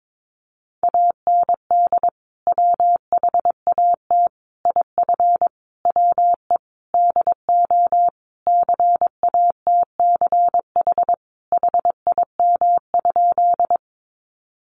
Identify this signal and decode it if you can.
Morse code, heard as a keyed tone, about 22 wpm, 700 Hz. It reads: AND WHAT IF WE DO CATCH HIM?
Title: Sentences from Top 1500 Words 22wpm